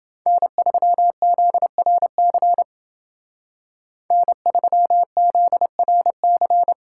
Morse code, heard as a keyed tone, about 29 wpm, 700 Hz. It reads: N3ZRC N3ZRC